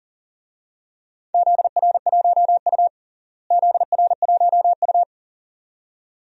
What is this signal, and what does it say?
Morse code, heard as a keyed tone, about 37 wpm, 700 Hz. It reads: ZR1U ZR1U
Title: DXCC-100 Call Signs - 4 Characters - Repeated 2x - 40wpm